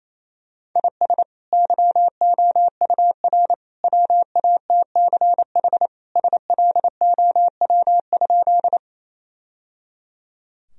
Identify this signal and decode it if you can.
Morse code, heard as a keyed tone, about 28 wpm, 700 Hz. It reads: IS YOUR WATCH SLOW?
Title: Sentences from Top 1000 Words - Review Part 2 - 28wpm